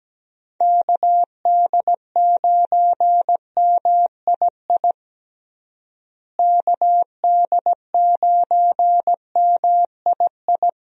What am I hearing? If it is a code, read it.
Morse code, heard as a keyed tone, about 17 wpm, 700 Hz. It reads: KD9MII KD9MII
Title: USA Call Signs - 6 Characters - Repeated 2x - 17wpm